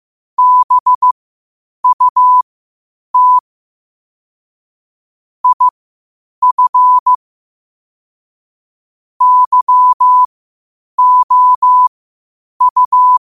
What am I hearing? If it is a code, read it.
Morse code, heard as a keyed tone, about 15 wpm, spaced as for 8 wpm, 1000 Hz. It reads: BUT IF YOU
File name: qotdfiles / 8wpm